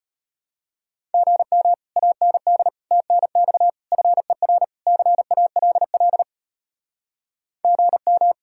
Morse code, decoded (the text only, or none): GM AND TNX FER CALL GM